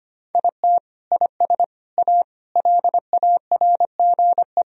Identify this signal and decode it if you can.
Morse code, heard as a keyed tone, about 25 wpm, 700 Hz. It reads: IT IS A LARGE
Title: Sentences from Top 1000 Words - Review Part 1 - 25wpm